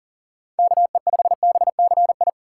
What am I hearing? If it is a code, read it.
Morse code, heard as a keyed tone, about 40 wpm, 700 Hz. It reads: KE5BCI
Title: USA Call Signs - 6 Characters - 3x Character Spacing - 40wpm